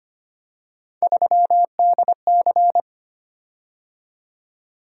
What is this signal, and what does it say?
Morse code, heard as a keyed tone, about 25 wpm, 700 Hz. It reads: 3DC
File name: Three Letter-Number Combinations - 25wpm